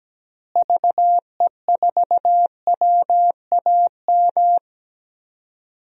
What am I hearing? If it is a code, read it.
Morse code, heard as a keyed tone, about 17 wpm, 700 Hz. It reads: VE4WAM
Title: DXCC-100 Call Signs - 6 Characters - Repeated 3x - 17wpm